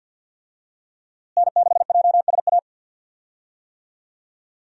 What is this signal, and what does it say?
Morse code, heard as a keyed tone, about 53 wpm, 700 Hz. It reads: N6JSA
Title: USA Call Signs - 5 Characters - Repeated 3x - 50wpm